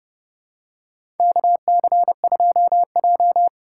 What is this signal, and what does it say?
Morse code, heard as a keyed tone, about 30 wpm, 700 Hz. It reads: KC2J
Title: USA Call Signs - 4 Characters - 3x Character Spacing - 30wpm